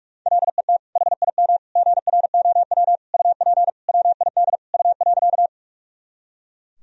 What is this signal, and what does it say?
Morse code, heard as a keyed tone, about 43 wpm, 700 Hz. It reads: LET HIM GROW UP WID U.
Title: Sentences from Top 300 Words and Abbreviations 45wpm